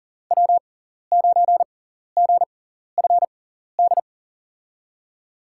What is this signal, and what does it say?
Morse code, heard as a keyed tone, about 39 wpm, spaced as for 14 wpm, 700 Hz. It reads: W9GFD
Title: USA Call Signs - 6x Character Spacing - 40wpm